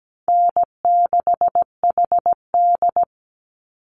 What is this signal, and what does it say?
Morse code, heard as a keyed tone, about 17 wpm, 700 Hz. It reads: N6HD
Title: Easy Call Signs Repeated 3x - 17wpm